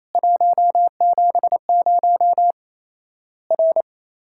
Morse code, heard as a keyed tone, about 28 wpm, 700 Hz. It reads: 170 R
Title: Worked All Europe DX Contest - QTC Exchange - 28wpm